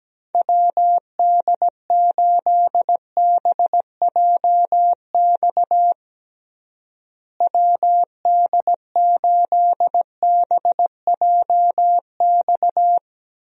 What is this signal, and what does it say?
Morse code, heard as a keyed tone, about 17 wpm, 700 Hz. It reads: WD8BJX WD8BJX
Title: USA Call Signs - 6 Characters - Repeated 3x - 17wpm